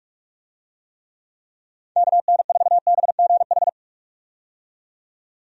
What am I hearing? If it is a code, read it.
Morse code, heard as a keyed tone, about 41 wpm, 700 Hz. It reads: KN4BGH
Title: USA Call Signs - 6 Characters - 45wpm